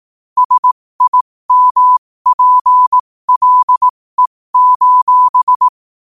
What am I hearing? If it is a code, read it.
Morse code, heard as a keyed tone, about 18 wpm, spaced as for 15 wpm, 1000 Hz. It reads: SIMPLE:
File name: qotdfiles / 15wpm